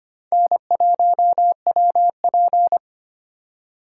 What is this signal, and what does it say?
Morse code, heard as a keyed tone, about 33 wpm, 700 Hz. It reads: N1WP